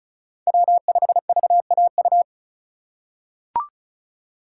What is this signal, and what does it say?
Morse code, heard as a keyed tone, about 34 wpm, 700 Hz. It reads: W5VAU
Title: USA Call Signs - 5 Characters - 3x Character Spacing - 35wpm